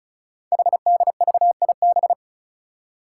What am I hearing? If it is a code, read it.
Morse code, heard as a keyed tone, about 36 wpm, 700 Hz. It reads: HDVIB
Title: Code Groups - 5 Letters - 35wpm